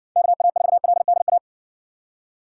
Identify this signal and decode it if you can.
Morse code, heard as a keyed tone, about 60 wpm, 700 Hz. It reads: KA4LDA